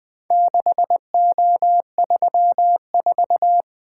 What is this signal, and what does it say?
Morse code, heard as a keyed tone, about 20 wpm, 700 Hz. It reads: 6O34
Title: Four Letter-Number Combinations - 20wpm